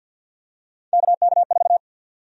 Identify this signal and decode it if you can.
Morse code, heard as a keyed tone, about 53 wpm, 700 Hz. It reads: KK4
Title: USA Call Sign Prefixes - 50wpm